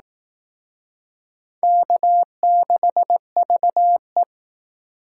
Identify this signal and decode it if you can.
Morse code, heard as a keyed tone, about 19 wpm, 700 Hz. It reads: K6VE